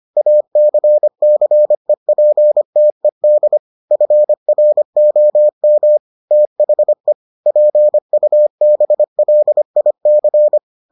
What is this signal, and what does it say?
Morse code, heard as a keyed tone, about 25 wpm, 600 Hz. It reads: ACCEPTED FROM THE PUBLIC